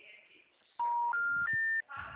Special information tone.
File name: out- -2010- - - .64159